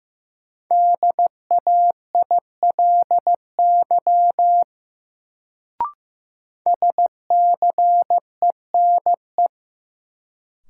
Morse code, heard as a keyed tone, about 15 wpm, 700 Hz. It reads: DAILY SCENE
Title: Sentences from Top 1100 Words - New Word Drill - 15wpm